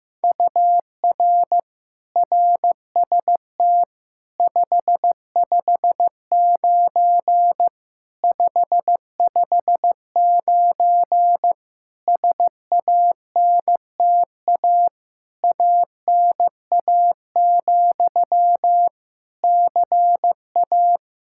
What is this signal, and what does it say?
Morse code, heard as a keyed tone, about 15 wpm, 700 Hz. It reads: UR RST 559 559 SANTA ANA, CA
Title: Practice QSO - Exchange 1 - Beginning - 15wpm